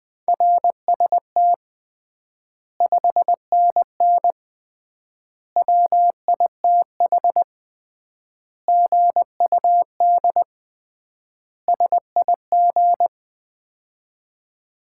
Morse code, heard as a keyed tone, about 20 wpm, 700 Hz. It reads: RST 5NN WITH GUD SIG
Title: Common QSO Phrases - 3x Word-Spacing - 20wpm